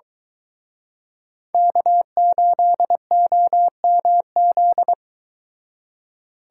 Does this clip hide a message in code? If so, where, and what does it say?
Yes: Morse code from 1.5 to 4.9 s, heard as a keyed tone, about 22 wpm, 700 Hz. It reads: K8OMZ